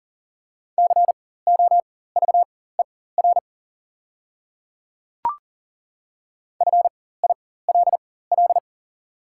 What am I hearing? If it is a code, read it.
Morse code, heard as a keyed tone, about 38 wpm, spaced as for 18 wpm, 700 Hz. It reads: COVER FILL
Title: 4x Character Spacing - Top 500 Words - 40wpm